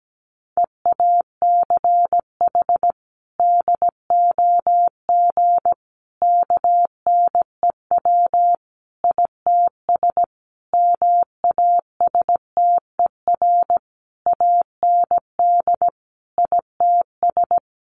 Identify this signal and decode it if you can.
Morse code, heard as a keyed tone, about 17 wpm, 700 Hz. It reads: EACH DOG KNEW ITS MASTER AND ITS
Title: Sentences from Top 1300 Words 17wpm